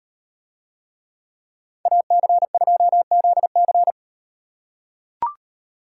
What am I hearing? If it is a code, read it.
Morse code, heard as a keyed tone, about 38 wpm, 700 Hz. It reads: AC2ZC